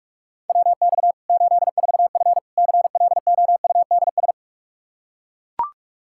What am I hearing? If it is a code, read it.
Morse code, heard as a keyed tone, about 42 wpm, 700 Hz. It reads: WX 84F CLOUDS
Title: Practice QSO - Exchange 2 - Info - 45wpm